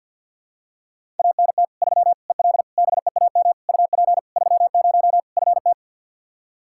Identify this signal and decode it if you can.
Morse code, heard as a keyed tone, about 52 wpm, 700 Hz. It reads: ANT 3 EL BEAM UP 30 FT
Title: Practice QSO - Exchange 2 - Info - 50wpm